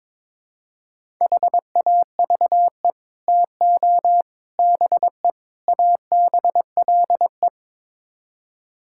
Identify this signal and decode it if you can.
Morse code, heard as a keyed tone, about 22 wpm, 700 Hz. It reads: HAVE TO BE ABLE
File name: practice / 22wpm / Build A Phrase - 5-Word Phrases - 22wpm